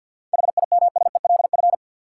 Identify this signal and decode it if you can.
Morse code, heard as a keyed tone, about 50 wpm, 700 Hz. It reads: HIMSELF
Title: Sentences from Top 1500 Words - New Word Drill - 50wpm